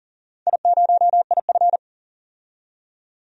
Morse code, heard as a keyed tone, about 39 wpm, 700 Hz. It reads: I0IF
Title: DXCC-100 Call Signs - 4 Characters - 40wpm